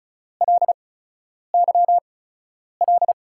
Morse code, heard as a keyed tone, about 36 wpm, spaced as for 10 wpm, 700 Hz. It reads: LYL
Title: USA Call Signs - 8x Character Spacing - 35wpm